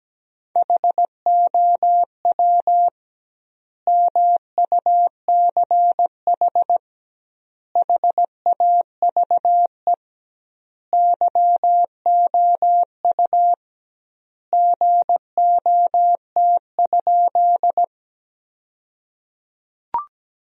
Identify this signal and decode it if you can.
Morse code, heard as a keyed tone, about 17 wpm, 700 Hz. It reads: HOW MUCH HAVE YOU GOT?
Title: 2x Word-Spacing - Sentences from Top 300 Words - 17wpm